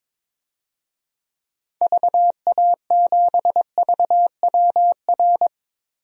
Morse code, heard as a keyed tone, about 22 wpm, 700 Hz. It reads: VA7VWR
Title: DXCC-100 Call Signs - 6 Characters - Repeated 2x - 22wpm